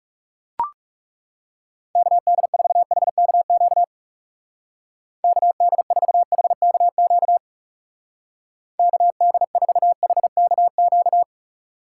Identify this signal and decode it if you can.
Morse code, heard as a keyed tone, about 37 wpm, 700 Hz. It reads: KD4HKQ KD4HKQ KD4HKQ
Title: Speed Racing - USA 2x3 Call Signs 30wpm